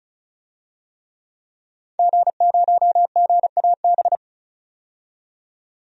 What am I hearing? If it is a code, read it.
Morse code, heard as a keyed tone, about 36 wpm, 700 Hz. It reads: G0GAB
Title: DXCC-100 Call Signs - 5 Characters - 35wpm